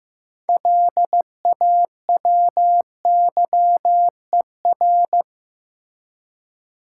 Morse code, heard as a keyed tone, about 15 wpm, 700 Hz. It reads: LAWYER
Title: Six Letter Words - 15wpm